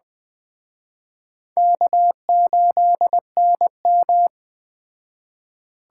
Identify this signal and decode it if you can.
Morse code, heard as a keyed tone, about 20 wpm, 700 Hz. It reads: K8NM
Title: Speed Racing - USA 1x2 Call Signs 17wpm